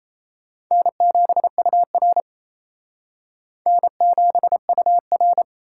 Morse code, heard as a keyed tone, about 32 wpm, 700 Hz. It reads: N7UR N7UR
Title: Speed Racing - USA Call Signs 28wpm